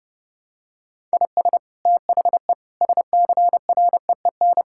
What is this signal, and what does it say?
Morse code, heard as a keyed tone, about 30 wpm, 700 Hz. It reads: IS THE SCREEN
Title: Sentences from Top 1000 Words - Review Part 6 - 30wpm